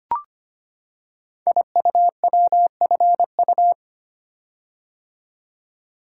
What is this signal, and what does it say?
Morse code, heard as a keyed tone, about 25 wpm, 700 Hz. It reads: IUWFU
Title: Code Groups - 5 Letters - 25wpm